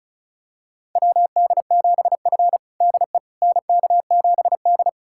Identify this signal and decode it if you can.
Morse code, heard as a keyed tone, about 36 wpm, 700 Hz. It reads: WD7F DE NK7D
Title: USA Callsign DE Callsign - 4 Characters - 35wpm